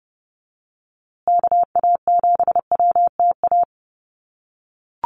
Morse code, heard as a keyed tone, about 30 wpm, 700 Hz. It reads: KA7WTA
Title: USA Call Signs Repeated 3x 30wpm